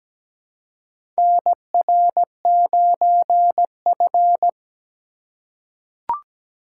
Morse code, heard as a keyed tone, about 17 wpm, 700 Hz. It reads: NR9F
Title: USA Call Signs - 4 Characters - 2x Character Spacing - 17wpm